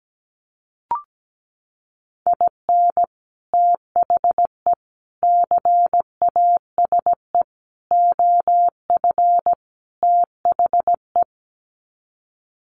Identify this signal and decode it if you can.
Morse code, heard as a keyed tone, about 17 wpm, 700 Hz. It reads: IN THE CASE OF THE
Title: Sets of 5 Words 17wpm